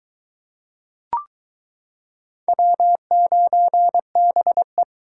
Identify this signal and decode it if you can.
Morse code, heard as a keyed tone, about 22 wpm, 700 Hz. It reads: W9BE